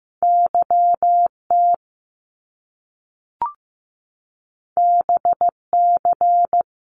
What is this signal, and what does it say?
Morse code, heard as a keyed tone, about 15 wpm, 700 Hz. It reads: YT BC
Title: Canadian Provinces and Territories - Repeated 3x - 15wpm